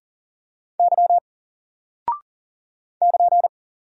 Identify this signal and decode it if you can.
Morse code, heard as a keyed tone, about 39 wpm, 700 Hz. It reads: Y <KN>
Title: Single Character (Letter:Number:Punctuation:Prosign) 40wpm